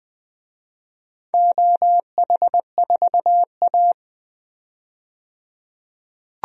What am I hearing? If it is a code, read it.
Morse code, heard as a keyed tone, about 20 wpm, 700 Hz. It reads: OH4A
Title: DXCC-100 Call Signs - 4 Characters - 20wpm